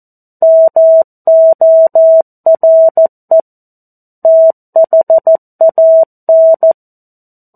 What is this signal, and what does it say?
Morse code, heard as a keyed tone, about 14 wpm, 650 Hz. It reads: MORE THAN
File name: 02-Chapter-02 14 WPM